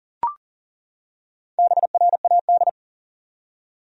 Morse code, heard as a keyed tone, about 40 wpm, 700 Hz. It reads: BRAD
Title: USA Common Names - Max 5 Letters - 40wpm